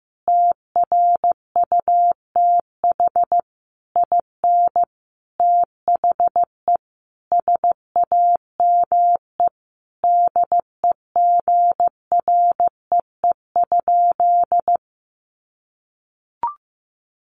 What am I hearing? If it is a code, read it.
Morse code, heard as a keyed tone, about 15 wpm, 700 Hz. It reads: TRUTH IN THE SAME DEGREE?
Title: Sentences from Top 1500 Words 15wpm